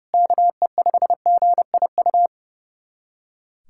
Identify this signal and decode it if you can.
Morse code, heard as a keyed tone, about 32 wpm, 700 Hz. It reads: KE5GIU
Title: USA Callsign DE Callsign - 6 Characters - 30wpm